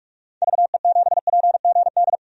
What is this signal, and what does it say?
Morse code, heard as a keyed tone, about 47 wpm, 700 Hz. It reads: VE7PGD
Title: DXCC-100 Call Signs - 6 Characters - 45wpm